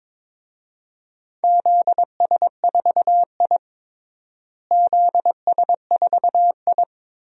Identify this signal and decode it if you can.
Morse code, heard as a keyed tone, about 22 wpm, 700 Hz. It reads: ZS4I ZS4I
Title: DXCC-100 Call Signs - 4 Characters - Repeated 3x - 22wpm